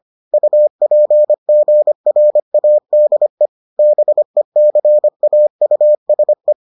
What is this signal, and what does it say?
Morse code, heard as a keyed tone, about 25 wpm, 600 Hz. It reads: UPGRADE BECAUSE